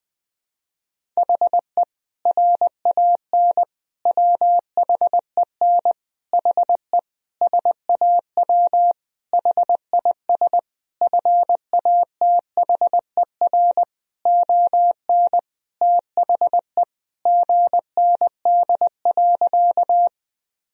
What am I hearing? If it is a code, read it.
Morse code, heard as a keyed tone, about 20 wpm, 700 Hz. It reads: HE RAN WHEN HE SAW HIS FATHER ON THE GND.
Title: Sentences from Top 500 Words and Abbreviations 20wpm